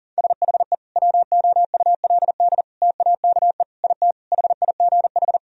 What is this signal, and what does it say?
Morse code, heard as a keyed tone, about 40 wpm, 700 Hz. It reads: SHE WOULD TAKE IT HIGH
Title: Sentences from Top 1000 Words - Review Part 1 - 40wpm